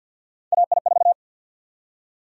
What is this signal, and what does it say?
Morse code, heard as a keyed tone, about 47 wpm, 700 Hz. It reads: AI4
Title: USA Call Sign Prefixes - 50wpm